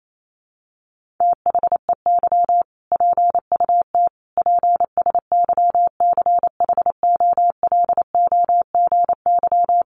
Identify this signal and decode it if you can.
Morse code, heard as a keyed tone, about 28 wpm, 700 Hz. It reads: THEY PUT PSYCHOLOGY
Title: Sentences from Top 1400 Words 28wpm